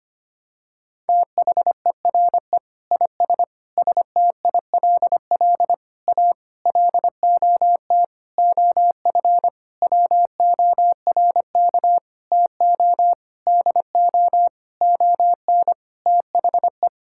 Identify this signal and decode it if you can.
Morse code, heard as a keyed tone, about 25 wpm, 700 Hz. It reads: THERE IS STILL A LOT OF WORK TO DO ON THE
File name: practice / 25wpm / Sentences with 9-Word Phrases - 25wpm